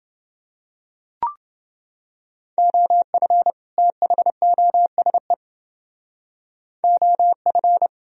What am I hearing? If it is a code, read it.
Morse code, heard as a keyed tone, about 28 wpm, 700 Hz. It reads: OF THOSE OF